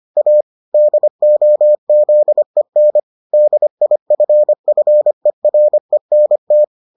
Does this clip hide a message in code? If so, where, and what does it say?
Yes: Morse code from 0.2 to 6.6 s, heard as a keyed tone, about 25 wpm, 600 Hz. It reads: A DOZEN DIFFERENT